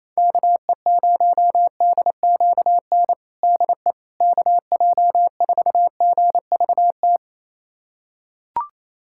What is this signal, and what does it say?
Morse code, heard as a keyed tone, about 28 wpm, 700 Hz. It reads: KE0DQN DE KJ4GVT
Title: USA Callsign DE Callsign - 6 Characters - 28wpm